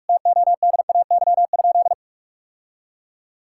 Morse code, heard as a keyed tone, about 46 wpm, 700 Hz. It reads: TODAY?
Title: Sentences from Top 1000 Words - Review Part 5 - 45wpm